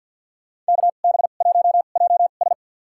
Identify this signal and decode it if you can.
Morse code, heard as a keyed tone, about 53 wpm, spaced as for 37 wpm, 700 Hz. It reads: KB1JS
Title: USA Call Signs - 2x Character Spacing - 50wpm